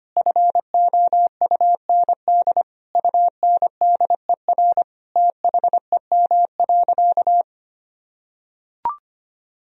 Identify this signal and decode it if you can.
Morse code, heard as a keyed tone, about 25 wpm, 700 Hz. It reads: FOUND UNDER THEM.
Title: Sentences from Top 1000 Words - Review Part 1 - 25wpm